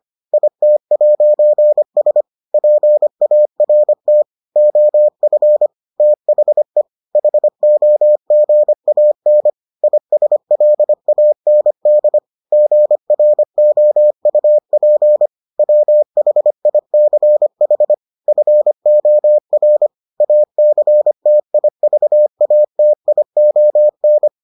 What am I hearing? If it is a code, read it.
Morse code, heard as a keyed tone, about 25 wpm, 600 Hz. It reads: IT'S PART OF THE HOGAN ISLAND GROUP WHICH FOR ACTIVATION